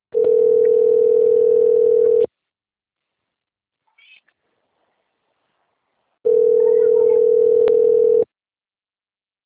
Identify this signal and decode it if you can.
Ringback tone (North America), 440 + 480 Hz.